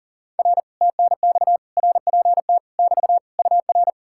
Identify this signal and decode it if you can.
Morse code, heard as a keyed tone, about 52 wpm, 700 Hz. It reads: R TNX RPT <BT> UR